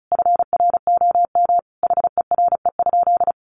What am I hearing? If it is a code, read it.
Morse code, heard as a keyed tone, about 34 wpm, 700 Hz. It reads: FROM HERE?
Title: Sentences from Top 1100 Words 35wpm